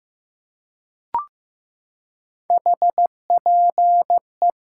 Morse code, heard as a keyed tone, about 15 wpm, 700 Hz. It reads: HPE
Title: Top 90 QSO Elements 15wpm